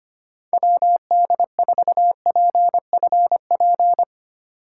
Morse code, heard as a keyed tone, about 32 wpm, 700 Hz. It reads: WD4PFP